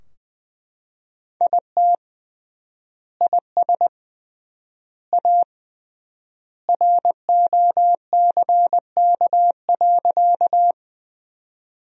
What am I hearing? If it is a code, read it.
Morse code, heard as a keyed tone, about 20 wpm, 700 Hz. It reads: IT IS A ROCK.
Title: 3x Word-Spacing - Sentences from Top 400 Words - 20wpm